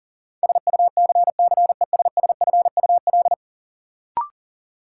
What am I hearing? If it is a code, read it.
Morse code, heard as a keyed tone, about 40 wpm, 700 Hz. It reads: SUCCESSFUL
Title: Sentences from Top 900 Words - New Word Drill - 40wpm